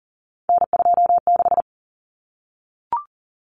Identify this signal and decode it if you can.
Morse code, heard as a keyed tone, about 40 wpm, 700 Hz. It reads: N26
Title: Common Cut Numbers - 3-Digit 40wpm